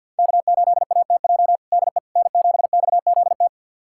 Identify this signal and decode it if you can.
Morse code, heard as a keyed tone, about 46 wpm, 700 Hz. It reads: K8ATJ DE N7XZT
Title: USA Callsign DE Callsign - 5 Characters - 50wpm